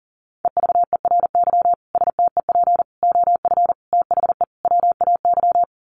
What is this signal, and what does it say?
Morse code, heard as a keyed tone, about 40 wpm, 700 Hz. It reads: EVERY STEP OF THE WAY
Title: Sets of 5 Words Encore2 40wpm